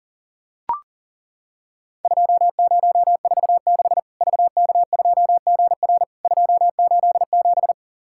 Morse code, heard as a keyed tone, about 40 wpm, 700 Hz. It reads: 2046 VK2GR 287
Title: Worked All Europe DX Contest - QTC Exchange - 40wpm